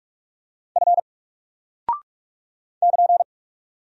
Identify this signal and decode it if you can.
Morse code, heard as a keyed tone, about 44 wpm, 700 Hz. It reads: F <KN>
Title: Single Character (Letter:Number:Punctuation:Prosign) Alt 45wpm